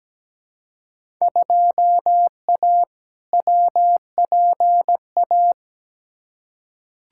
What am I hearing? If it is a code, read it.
Morse code, heard as a keyed tone, about 17 wpm, 700 Hz. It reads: 2A WPA
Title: Field Day Exchanges - 17wpm